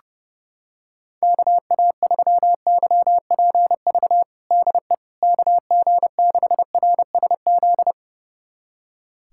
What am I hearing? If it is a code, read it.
Morse code, heard as a keyed tone, about 32 wpm, 700 Hz. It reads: KA3YPV DE KG6RSZ
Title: USA Callsign DE Callsign 30wpm